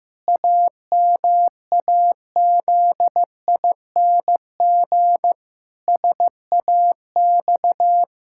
Morse code, heard as a keyed tone, about 16 wpm, 700 Hz. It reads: AMAZING SAX
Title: Pangram Challenge v2 - 15wpm